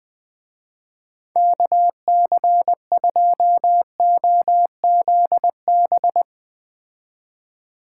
Morse code, heard as a keyed tone, about 20 wpm, 700 Hz. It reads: KC2OZB
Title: Speed Racing - USA 2x3 Call Signs 17wpm